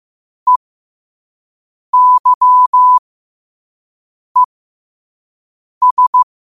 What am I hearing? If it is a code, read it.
Morse code, heard as a keyed tone, about 15 wpm, spaced as for 5 wpm, 1000 Hz. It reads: EYES
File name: qotdfiles / 5wpm